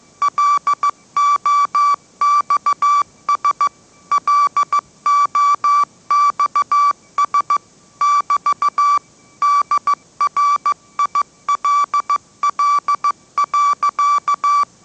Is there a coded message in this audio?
Yes: Morse code from 0.2 to 14.6 s, heard as a keyed tone, about 15 wpm, 1160 Hz. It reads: LOXS LOXS <BT> DRILL.